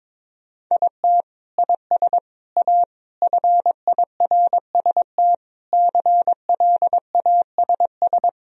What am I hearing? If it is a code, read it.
Morse code, heard as a keyed tone, about 22 wpm, 700 Hz. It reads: IT IS A FIRST CLASS
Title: Sentences from Top 1000 Words - Review Part 2 - 22wpm